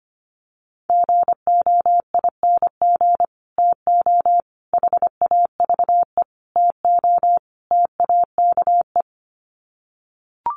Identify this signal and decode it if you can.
Morse code, heard as a keyed tone, about 25 wpm, 700 Hz. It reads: GOING TO HAVE TO TAKE
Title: Sets of 5 Words Encore 25wpm